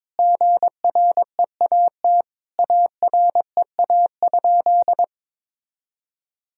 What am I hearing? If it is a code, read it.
Morse code, heard as a keyed tone, about 22 wpm, 700 Hz. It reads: GREAT AREA?
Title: Sentences from Top 1000 Words - Review Part 2 - 22wpm